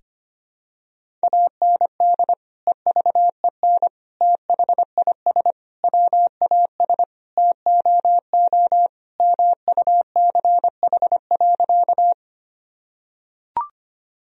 Morse code, heard as a keyed tone, about 25 wpm, 700 Hz. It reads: AND EVEN THIS WAS TOO MUCH.
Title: Sentences from Top 1000 Words - Review Part 1 - 25wpm